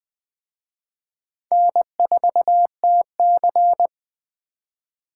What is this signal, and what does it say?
Morse code, heard as a keyed tone, about 20 wpm, 700 Hz. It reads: N4TC
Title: Speed Racing - USA 1x2 Call Signs 17wpm